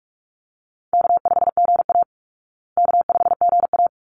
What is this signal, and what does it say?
Morse code, heard as a keyed tone, about 44 wpm, 700 Hz. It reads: K5GA K5GA
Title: Easy Call Signs Repeated 2x - 45wpm